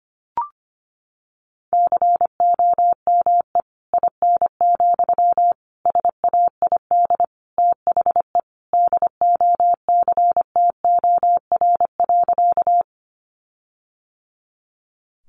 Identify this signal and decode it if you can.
Morse code, heard as a keyed tone, about 25 wpm, 700 Hz. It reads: COME IN, SAID THE DOCTOR.
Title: Sentences from Top 1500 Words 25wpm